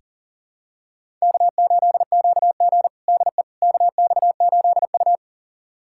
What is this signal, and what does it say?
Morse code, heard as a keyed tone, about 40 wpm, 700 Hz. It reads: K8QG DE KX8U
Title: USA Callsign DE Callsign - 4 Characters - 40wpm